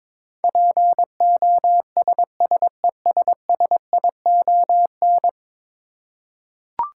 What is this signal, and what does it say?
Morse code, heard as a keyed tone, about 22 wpm, 700 Hz. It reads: POSSESSION
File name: practice / 22wpm / 10 Letter Words - First 3 Letters Given - 22wpm